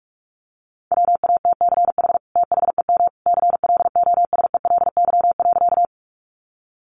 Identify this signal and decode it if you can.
Morse code, heard as a keyed tone, about 44 wpm, 700 Hz. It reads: WATCH THEM CLOSELY.
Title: Sentences from Top 1400 Words 45wpm